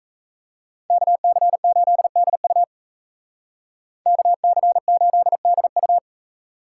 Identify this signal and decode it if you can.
Morse code, heard as a keyed tone, about 33 wpm, 700 Hz. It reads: KC8DU KC8DU
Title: Speed Racing - USA 2x2 Call Signs 28wpm